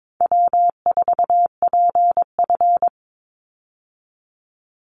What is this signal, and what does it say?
Morse code, heard as a keyed tone, about 22 wpm, 700 Hz. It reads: W4PF
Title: Easy Call Signs - 22wpm